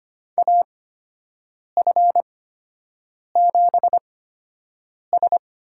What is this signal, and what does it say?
Morse code, heard as a keyed tone, about 25 wpm, spaced as for 7 wpm, 700 Hz. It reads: AF7S
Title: USA Call Signs - 4 Characters - 8x Character Spacing - 25wpm